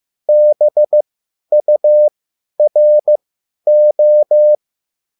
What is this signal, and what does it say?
Morse code, heard as a keyed tone, about 15 wpm, spaced as for 10 wpm, 600 Hz. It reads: BURO
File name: QSL / CW lesson